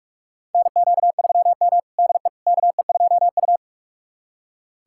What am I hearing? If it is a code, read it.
Morse code, heard as a keyed tone, about 46 wpm, 700 Hz. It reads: NQ3M DE KE2U
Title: USA Callsign DE Callsign - 4 Characters - 45wpm